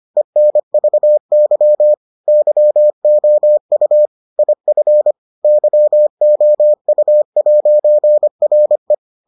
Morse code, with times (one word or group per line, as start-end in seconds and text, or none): 0.2-1.9 ENVY
2.3-4.1 YOU
4.4-5.1 IF
5.4-8.9 YOU'RE